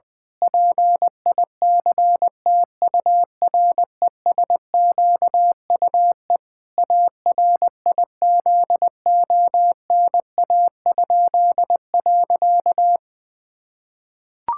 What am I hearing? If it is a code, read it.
Morse code, heard as a keyed tone, about 20 wpm, 700 Hz. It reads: PICTURESQUE ARIZONA?.
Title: Pangram Challenge v2 - 20wpm